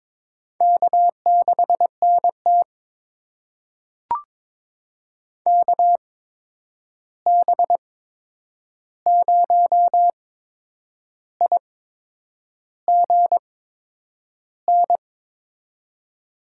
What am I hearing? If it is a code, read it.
Morse code, heard as a keyed tone, about 22 wpm, spaced as for 8 wpm, 700 Hz. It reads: K6NT KB0IGN
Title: USA Call Signs - 8x Character Spacing - 22wpm